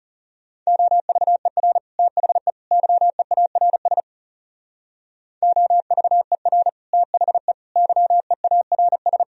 Morse code, heard as a keyed tone, about 36 wpm, 700 Hz. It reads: OVER THE YEARS OVER THE YEARS
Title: Speed-Racing - Sets of 3 Words 30wpm